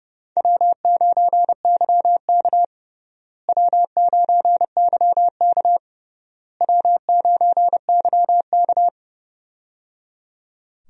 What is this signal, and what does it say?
Morse code, heard as a keyed tone, about 30 wpm, 700 Hz. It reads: W9YK W9YK W9YK
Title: USA Call Signs - 4 Characters - Repeated 3x - 30wpm